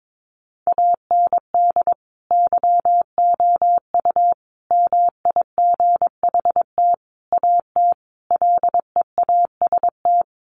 Morse code, heard as a keyed tone, about 22 wpm, 700 Hz. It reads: AND YOU MIGHT AT LEAST
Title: Sentences from Top 1500 Words 22wpm